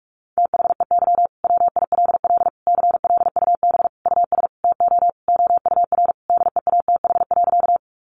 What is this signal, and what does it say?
Morse code, heard as a keyed tone, about 44 wpm, 700 Hz. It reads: THEY WILL CLUB US TO OUR DEATH.
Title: Sentences from Top 1200 Words 45wpm